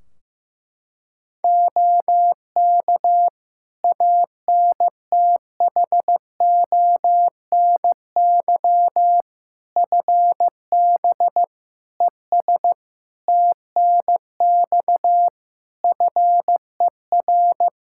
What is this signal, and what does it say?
Morse code, heard as a keyed tone, about 15 wpm, 700 Hz. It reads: OK ANTHONY FB ES TNX FER
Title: Practice QSO - Exchange 2 - Info - 15wpm